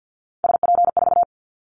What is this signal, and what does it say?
Morse code, heard as a keyed tone, about 50 wpm, 700 Hz. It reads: SP4
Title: Callsign Prefixes for Top 25 DXCC - 50wpm